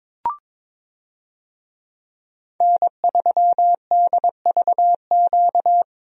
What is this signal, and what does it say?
Morse code, heard as a keyed tone, about 22 wpm, 700 Hz. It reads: N3DVQ
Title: USA Call Signs - 5 Characters - 22wpm